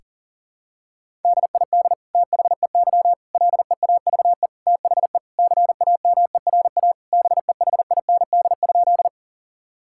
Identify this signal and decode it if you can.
Morse code, heard as a keyed tone, about 37 wpm, 700 Hz. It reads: DID THEY LEAVE THE CAMERA BEHIND?
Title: Sentences from Top 1000 Words - Review Part 5 - 40wpm